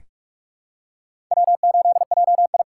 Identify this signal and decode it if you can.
Morse code, heard as a keyed tone, about 41 wpm, 700 Hz. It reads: W8JI
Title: USA Call Signs - 4 Characters - 3x Character Spacing - 45wpm